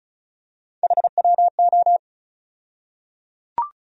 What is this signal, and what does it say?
Morse code, heard as a keyed tone, about 34 wpm, 700 Hz. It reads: HWO
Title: USA Call Sign Postfixes - 3 Letters 35wpm